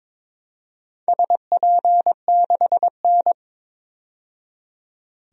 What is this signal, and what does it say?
Morse code, heard as a keyed tone, about 22 wpm, 700 Hz. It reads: SP6N
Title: DXCC-100 Call Signs - 4 Characters - 22wpm